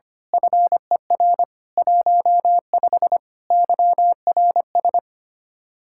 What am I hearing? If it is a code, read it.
Morse code, heard as a keyed tone, about 25 wpm, 700 Hz. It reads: FER 15 YRS
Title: Speed-Racing Common QSO Phrases Encore - 25wpm